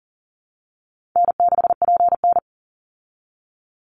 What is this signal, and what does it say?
Morse code, heard as a keyed tone, about 40 wpm, 700 Hz. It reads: N6PN
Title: Easy Call Signs Repeated 3x - 40wpm